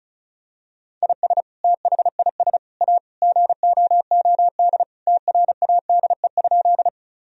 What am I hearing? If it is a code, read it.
Morse code, heard as a keyed tone, about 36 wpm, 700 Hz. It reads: IS THIS A GOOD TRADE?
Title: Sentences from Top 1000 Words - Review Part 4 - 35wpm